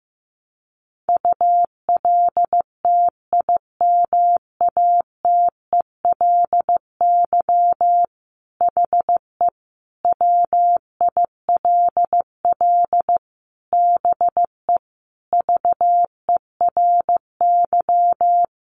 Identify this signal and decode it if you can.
Morse code, heard as a keyed tone, about 15 wpm, 700 Hz. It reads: ULTIMATELY HE WILL BE VERY
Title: Sentences from Top 1200 Words 15wpm